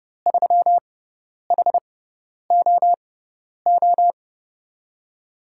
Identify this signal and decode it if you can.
Morse code, heard as a keyed tone, about 31 wpm, spaced as for 10 wpm, 700 Hz. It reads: 3HOO
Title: USA Call Signs - 5 Characters - 6x Character Spacing - 30wpm